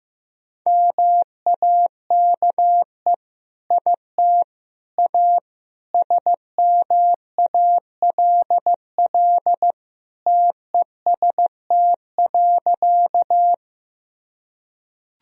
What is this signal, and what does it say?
Morse code, heard as a keyed tone, about 14 wpm, 700 Hz. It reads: MAKE IT A SMALL TEST.
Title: Sentences from Top 1000 Words - Review Part 3 - 15wpm